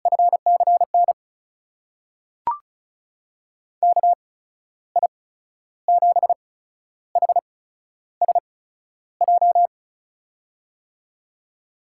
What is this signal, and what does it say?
Morse code, heard as a keyed tone, about 36 wpm, spaced as for 12 wpm, 700 Hz. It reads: FCN KI7HSJ
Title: USA Call Signs - 6 Characters - 8x Character Spacing - 35wpm